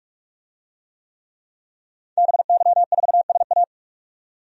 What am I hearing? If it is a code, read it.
Morse code, heard as a keyed tone, about 46 wpm, 700 Hz. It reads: BY4SA
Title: DXCC-100 Call Signs - 5 Characters - 45wpm